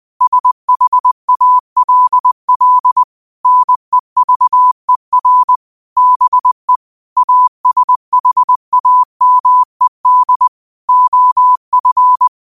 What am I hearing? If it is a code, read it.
Morse code, heard as a keyed tone, about 20 wpm, 1000 Hz. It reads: SHALL NEVER BE ASHAMED OF